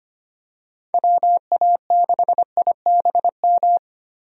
Morse code, heard as a keyed tone, about 25 wpm, 700 Hz. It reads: WA6IBM
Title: USA Call Signs - 6 Characters - Repeated 2x - 25wpm